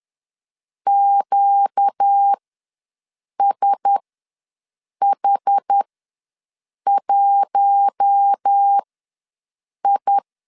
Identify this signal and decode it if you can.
Morse code, heard as a keyed tone, about 10 wpm, spaced as for 6 wpm, 800 Hz. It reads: QSH1I
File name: LEZIONI / 5 LBVC / 5 L ws 10wpm 3636